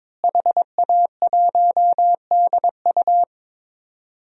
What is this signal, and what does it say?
Morse code, heard as a keyed tone, about 22 wpm, 700 Hz. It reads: HA1DU
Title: DXCC-100 Call Signs - 5 Characters - Repeated 2x - 22wpm